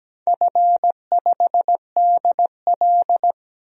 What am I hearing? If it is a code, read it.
Morse code, heard as a keyed tone, about 17 wpm, 700 Hz. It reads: F5DL
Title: DXCC-100 Call Signs - 4 Characters - 17wpm